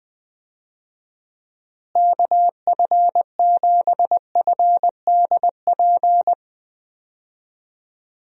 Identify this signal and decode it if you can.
Morse code, heard as a keyed tone, about 20 wpm, 700 Hz. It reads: KF7FDP
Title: USA Call Signs - 6 Characters - Repeated 3x - 20wpm